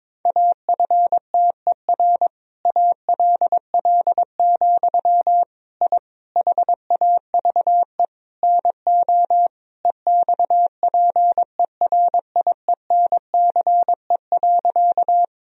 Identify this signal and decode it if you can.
Morse code, heard as a keyed tone, about 22 wpm, 690 Hz. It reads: AFTER ALL, I HAVE NO EXPERIENCE.
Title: Sentences from Top 1000 Words - Review Part 3 - 22wpm